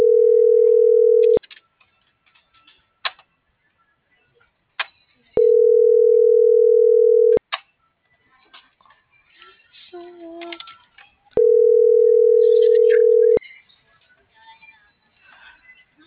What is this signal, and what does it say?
Ringback tone (North America), 440 + 480 Hz.